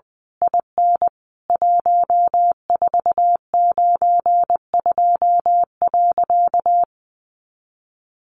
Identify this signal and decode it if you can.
Morse code, heard as a keyed tone, about 20 wpm, 700 Hz. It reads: IN 1492.
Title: A Sound Of Thunder 20wpm